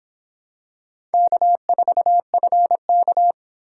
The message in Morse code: K4FK

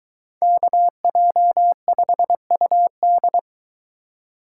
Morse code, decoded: KJ5UD

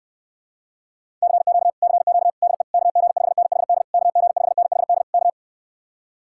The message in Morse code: CQ CQ DE KG5NHD KG5NHD K

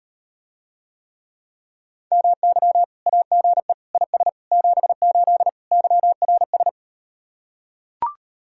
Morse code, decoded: MY AGE IS 78 YRS

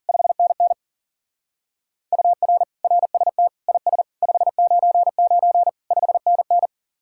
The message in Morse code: 5NN UR RST IS 599 5NN

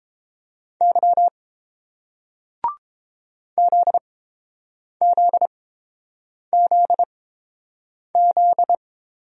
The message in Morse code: Y ZZZZ

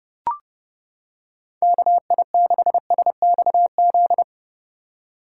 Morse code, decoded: KI6SXZ

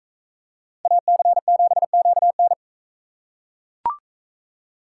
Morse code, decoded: AC7QN

